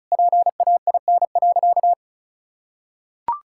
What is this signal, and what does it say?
Morse code, heard as a keyed tone, about 36 wpm, 700 Hz. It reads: PAIN.
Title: Sentences from Top 1600 Words 35wpm